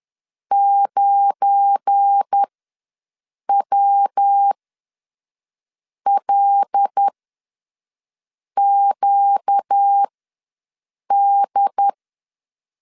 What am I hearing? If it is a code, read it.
Morse code, heard as a keyed tone, about 10 wpm, spaced as for 5 wpm, 800 Hz. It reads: 9WLQD
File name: LEZIONI / 8 YJ45 / 8 YJ ws 10wpm 4043